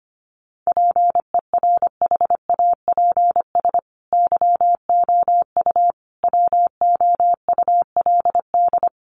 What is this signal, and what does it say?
Morse code, heard as a keyed tone, about 25 wpm, 700 Hz. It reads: PERHAPS YOU WOULD
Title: Sentences from Top 500 Words 25wpm